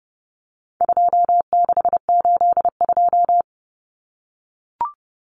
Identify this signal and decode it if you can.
Morse code, heard as a keyed tone, about 30 wpm, 700 Hz. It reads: 2682
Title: Common Cut Numbers - 4-Digit 30wpm